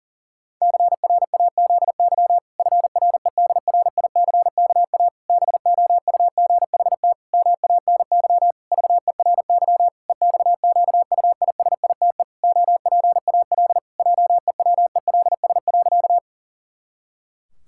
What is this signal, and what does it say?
Morse code, heard as a keyed tone, about 40 wpm, 700 Hz. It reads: CRAZY FREDRICKA BOUGHT MANY VERY EXQUISITE OPAL JEWELS.
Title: Pangram Challenge v2 - 40wpm